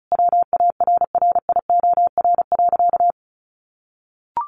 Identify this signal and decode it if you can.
Morse code, heard as a keyed tone, about 34 wpm, 700 Hz. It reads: WARRIOR.
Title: A Sound Of Thunder 35wpm